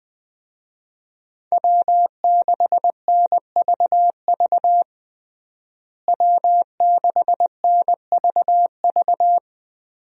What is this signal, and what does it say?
Morse code, heard as a keyed tone, about 20 wpm, 700 Hz. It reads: W6NVV W6NVV
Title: USA Call Signs - 5 Characters - Repeated 3x - 20wpm